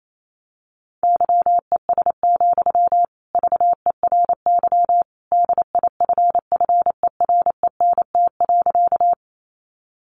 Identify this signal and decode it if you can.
Morse code, heard as a keyed tone, about 28 wpm, 700 Hz. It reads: YES, VERY DIFFERENT.
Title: Sentences from Top 600 Words 28wpm